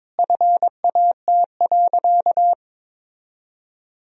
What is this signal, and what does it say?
Morse code, heard as a keyed tone, about 22 wpm, 700 Hz. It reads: FAT.
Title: Sentences from Top 1000 Words - Review Part 4 - 22wpm